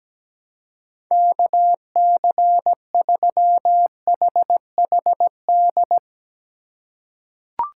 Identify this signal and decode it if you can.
Morse code, heard as a keyed tone, about 17 wpm, 700 Hz. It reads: KC3HHD
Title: USA Call Signs - 6 Characters - 8x Character Spacing - 17wpm